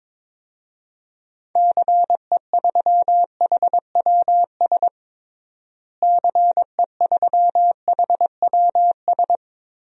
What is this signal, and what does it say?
Morse code, heard as a keyed tone, about 22 wpm, 700 Hz. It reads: CE3HWS CE3HWS
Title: DXCC-100 Call Signs - 6 Characters - Repeated 3x - 22wpm